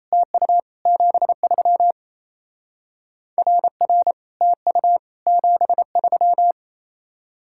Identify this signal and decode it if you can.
Morse code, heard as a keyed tone, about 33 wpm, 700 Hz. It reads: TU 73 RR TU 73
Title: Speed-Racing Common QSO Phrases - 28wpm